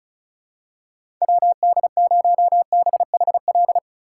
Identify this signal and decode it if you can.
Morse code, heard as a keyed tone, about 36 wpm, 700 Hz. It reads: WD0BHL
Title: USA Call Signs - 2x Character Spacing - 35wpm